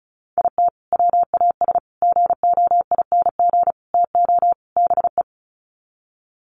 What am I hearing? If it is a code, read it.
Morse code, heard as a keyed tone, about 34 wpm, 700 Hz. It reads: IT WAS GOING TO BE
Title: Sets of 5 Words 35wpm